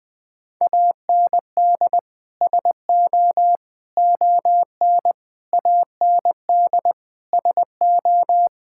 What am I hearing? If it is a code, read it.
Morse code, heard as a keyed tone, about 20 wpm, 700 Hz. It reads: AND SO ON AND SO
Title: Sets of 6 Words - 20wpm